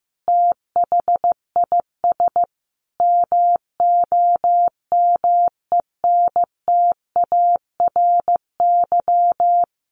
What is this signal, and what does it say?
Morse code, heard as a keyed tone, about 15 wpm, 700 Hz. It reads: THIS MOMENTARY
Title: A Sound Of Thunder 15wpm